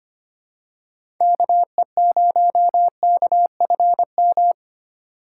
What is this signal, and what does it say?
Morse code, heard as a keyed tone, about 25 wpm, 700 Hz. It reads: KE0KFM
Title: USA Call Signs - 6 Characters - 3x Character Spacing - 25wpm